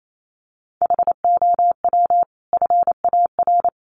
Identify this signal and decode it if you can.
Morse code, heard as a keyed tone, about 28 wpm, 700 Hz. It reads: HOW FAR
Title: Sentences from Top 1200 Words 28wpm